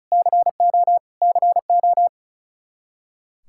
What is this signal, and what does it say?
Morse code, heard as a keyed tone, about 34 wpm, 700 Hz. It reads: CO CO
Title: Practice QSO - Exchange 1 - Beginning - 35wpm